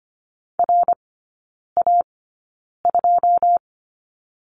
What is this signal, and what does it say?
Morse code, heard as a keyed tone, about 25 wpm, spaced as for 9 wpm, 690 Hz. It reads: RA2
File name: Lesson 15e - Number 5 - Three Characters 25wpm